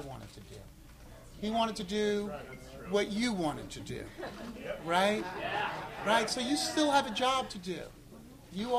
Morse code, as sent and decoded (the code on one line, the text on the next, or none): none